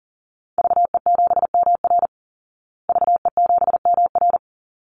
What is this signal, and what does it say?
Morse code, heard as a keyed tone, about 40 wpm, 700 Hz. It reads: VE7MR VE7MR
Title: Call Signs Repeated 3x - 40wpm